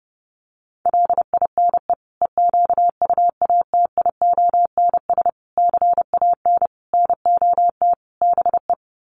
Morse code, heard as a keyed tone, about 30 wpm, 700 Hz. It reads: LINE EQUATIONS CAN NOT BE
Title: Sets of 5 Words 30wpm